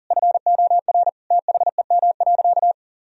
Morse code, heard as a keyed tone, about 40 wpm, 700 Hz. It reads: FOR THEM.
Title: Sentences from Top 1000 Words - Review Part 1 - 40wpm